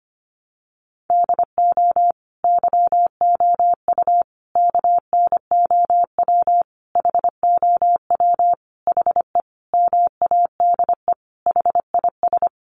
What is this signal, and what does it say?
Morse code, heard as a keyed tone, about 25 wpm, 700 Hz. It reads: DO YOU KNOW HOW HE MADE HIS
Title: Sentences from Top 300 Words 25wpm